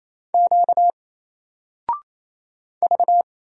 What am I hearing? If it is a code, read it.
Morse code, heard as a keyed tone, about 28 wpm, 700 Hz. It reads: Q V
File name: Single-Letter-Number-28wpm